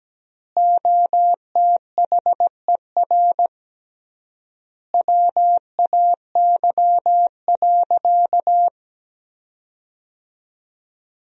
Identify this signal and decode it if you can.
Morse code, heard as a keyed tone, about 17 wpm, 700 Hz. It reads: OTHER WAY.
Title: 3x Word-Spacing - Sentences from Top 100 Words - 17wpm